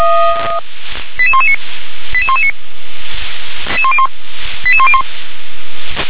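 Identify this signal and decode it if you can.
Five-tone selective call (ZVEI), heard as quick runs of five tones, tones of about 70 ms. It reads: 88119 88119 88101
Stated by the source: ZVEI five-tone selective call 88127 received on SDR.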